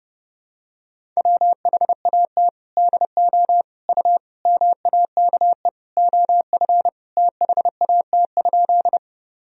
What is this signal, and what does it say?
Morse code, heard as a keyed tone, about 32 wpm, 700 Hz. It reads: WHAT DO U MAKE OF THAT?
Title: Sentences from Top 100 Words and Abbreviations 30wpm